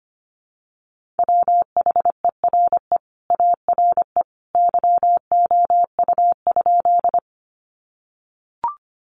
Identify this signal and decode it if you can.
Morse code, heard as a keyed tone, about 25 wpm, 700 Hz. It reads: WHERE ARE YOU?
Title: Sentences from Top 200 Words 25wpm